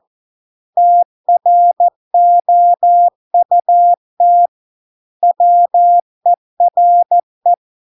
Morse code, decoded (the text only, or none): TROUT WERE